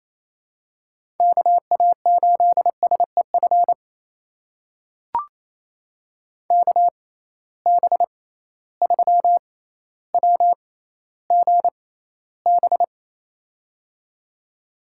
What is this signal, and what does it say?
Morse code, heard as a keyed tone, about 28 wpm, spaced as for 14 wpm, 700 Hz. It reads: KA8SEF KB3WGB
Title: USA Call Signs - 6 Characters - 6x Character Spacing - 28wpm